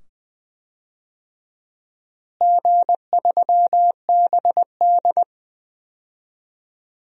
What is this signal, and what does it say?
Morse code, heard as a keyed tone, about 21 wpm, 700 Hz. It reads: G3BD
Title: DXCC-100 Call Signs - 4 Characters - Repeated 2x - 20wpm